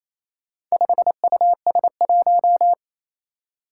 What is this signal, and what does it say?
Morse code, heard as a keyed tone, about 29 wpm, 700 Hz. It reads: 5US1
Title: Four Challenging Letter-Number Combinations 28wpm